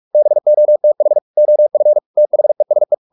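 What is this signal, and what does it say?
Morse code, heard as a keyed tone, about 44 wpm, 600 Hz. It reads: BOTH OF THESE
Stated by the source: Morse practice at 45 wpm, the novel Never by Ken Follett in English, from CW med Gnister.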